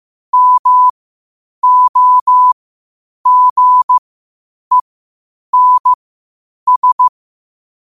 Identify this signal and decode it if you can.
Morse code, heard as a keyed tone, about 15 wpm, spaced as for 8 wpm, 1000 Hz. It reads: MOGENS